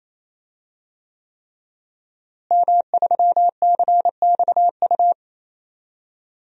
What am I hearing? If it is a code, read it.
Morse code, heard as a keyed tone, about 27 wpm, 700 Hz. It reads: M3CXU
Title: DXCC-100 Call Signs - 5 Characters - Repeated 3x - 28wpm